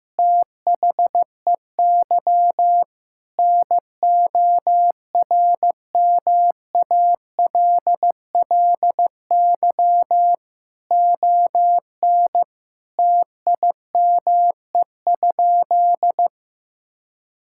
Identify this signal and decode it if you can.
Morse code, heard as a keyed tone, about 15 wpm, 700 Hz. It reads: THEY NORMALLY ON TIME?
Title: Sentences from Top 1000 Words - Review Part 6 - 15wpm